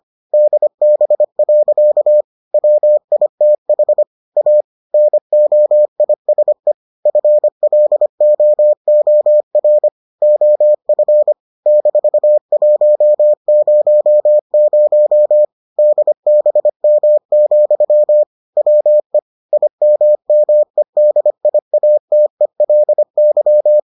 Morse code, heard as a keyed tone, about 25 wpm, 600 Hz. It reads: DB. WITH A NOISE FLOOR OF -100 DBM, WE IMMEDIATELY